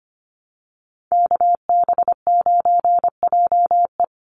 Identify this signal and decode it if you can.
Morse code, heard as a keyed tone, about 24 wpm, 700 Hz. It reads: KB9JE
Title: USA Call Signs 25wpm